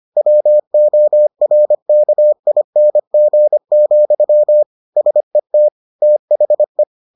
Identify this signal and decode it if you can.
Morse code, heard as a keyed tone, about 25 wpm, 600 Hz. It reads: WORKING, SET THE